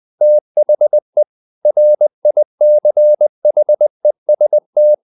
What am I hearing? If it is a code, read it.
Morse code, heard as a keyed tone, about 20 wpm, 600 Hz. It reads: THE RICHEST
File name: CW / Practice / EnglishQuotations0320